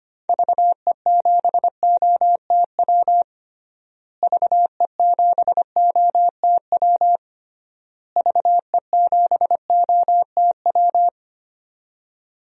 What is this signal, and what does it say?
Morse code, heard as a keyed tone, about 25 wpm, 700 Hz. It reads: VE7OTW VE7OTW VE7OTW
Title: DXCC-100 Call Signs - 6 Characters - Repeated 3x - 25wpm